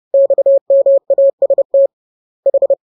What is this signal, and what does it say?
Morse code, heard as a keyed tone, about 30 wpm, 550 Hz. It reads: XMAST H